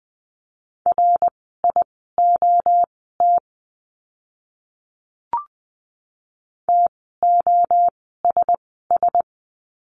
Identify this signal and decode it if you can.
Morse code, heard as a keyed tone, about 20 wpm, spaced as for 14 wpm, 700 Hz. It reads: RIOT TOSS
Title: 061 Lesson 11f - Letter h - Words 20wpm